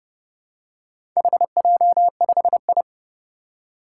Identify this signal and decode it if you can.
Morse code, heard as a keyed tone, about 30 wpm, 700 Hz. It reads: HJ5I
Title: Four Challenging Letter-Number Combinations 30wpm